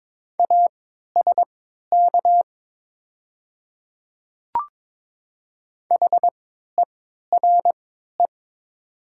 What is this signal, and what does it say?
Morse code, heard as a keyed tone, about 22 wpm, spaced as for 13 wpm, 690 Hz. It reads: ASK HERE
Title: 3x Character Spacing - Top 200 Words - 22wpm